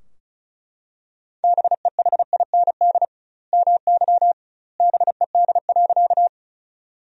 Morse code, heard as a keyed tone, about 36 wpm, 700 Hz. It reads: BEHIND MY BED.
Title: 2x Word-Spacing - Sentences from Top 500 Words - 35wpm